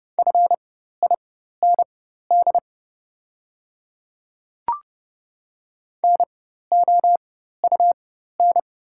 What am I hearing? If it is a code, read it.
Morse code, heard as a keyed tone, about 29 wpm, spaced as for 14 wpm, 690 Hz. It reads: FIND NOUN
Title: 4x Character Spacing - Top 500 Words - 30wpm